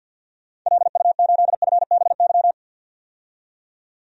Morse code, heard as a keyed tone, about 51 wpm, 700 Hz. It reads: LU8FBY
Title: DXCC-100 Call Signs - 6 Characters - 50wpm